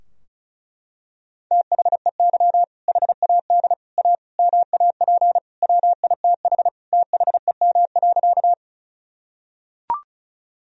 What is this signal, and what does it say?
Morse code, heard as a keyed tone, about 36 wpm, 700 Hz. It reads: THEY HAD A MAP WITH THEM.
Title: Sentences from Top 1000 Words - Review Part 2 - 35wpm